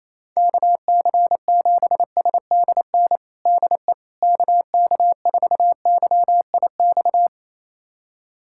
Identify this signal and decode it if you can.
Morse code, heard as a keyed tone, about 28 wpm, 700 Hz. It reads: KC7SDN DE KK4YIX
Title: USA Callsign DE Callsign - 6 Characters - 28wpm